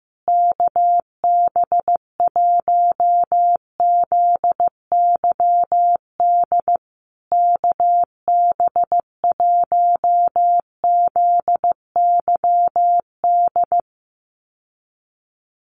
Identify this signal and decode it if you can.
Morse code, heard as a keyed tone, about 15 wpm, 700 Hz. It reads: KB1ZYD KB1ZYD
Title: USA Call Signs Repeated 3x 15wpm